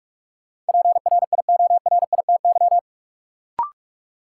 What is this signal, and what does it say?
Morse code, heard as a keyed tone, about 46 wpm, 700 Hz. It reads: PRIORITY
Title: Sentences from Top 1600 Words - New Word Drill - 45wpm